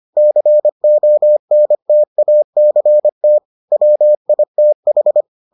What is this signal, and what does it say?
Morse code, heard as a keyed tone, about 25 wpm, 600 Hz. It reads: CONTACT WITH